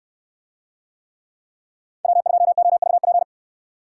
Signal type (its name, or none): Morse code